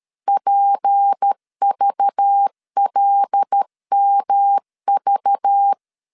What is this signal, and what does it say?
Morse code, heard as a keyed tone, about 13 wpm, 800 Hz. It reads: PVLMV
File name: LEZIONI / 6 PKX0 / 6 P 12wpm 2033